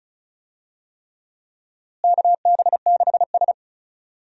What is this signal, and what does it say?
Morse code, heard as a keyed tone, about 36 wpm, 700 Hz. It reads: KB6S
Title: USA Call Signs - 4 Characters - Repeated 3x - 35wpm